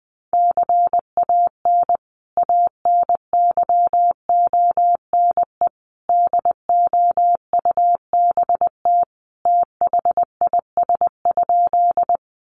Morse code, heard as a keyed tone, about 20 wpm, 700 Hz. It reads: CAN ANYONE DOUBT THIS?
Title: Sentences from Top 1500 Words 20wpm